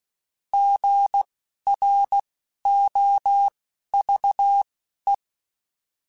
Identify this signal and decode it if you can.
Morse code, heard as a keyed tone, about 16 wpm, spaced as for 11 wpm, 800 Hz. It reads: GROVE